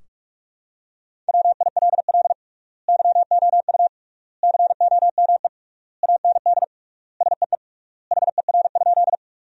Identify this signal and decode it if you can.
Morse code, heard as a keyed tone, about 46 wpm, 700 Hz. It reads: WILL YOU COME AND SEE HER?
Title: 3x Word-Spacing - Sentences from Top 100 Words - 45wpm